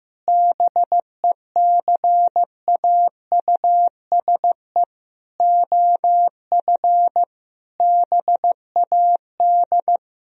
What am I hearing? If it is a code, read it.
Morse code, heard as a keyed tone, about 15 wpm, 700 Hz. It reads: BECAUSE OF BAD
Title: Sentences from Top 1000 Words - Review Part 4 - 15wpm